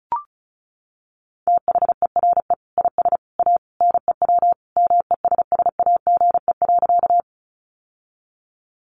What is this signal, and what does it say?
Morse code, heard as a keyed tone, about 34 wpm, 700 Hz. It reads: THERE IS A NEW MESSAGE.
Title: Sentences from Top 1100 Words 35wpm